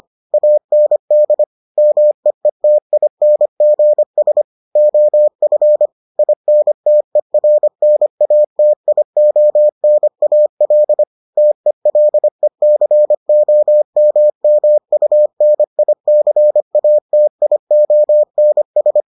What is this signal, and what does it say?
Morse code, heard as a keyed tone, about 25 wpm, 600 Hz. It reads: AND MEETINGS OF INTERNATIONAL TELECOMMUNICATIONS